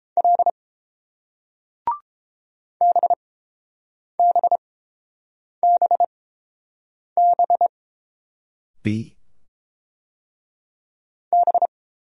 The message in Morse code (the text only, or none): L BBBB B